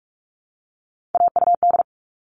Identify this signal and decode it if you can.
Morse code, heard as a keyed tone, about 44 wpm, 700 Hz. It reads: AUD
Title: All Cut Numbers - 3-Digit 45wpm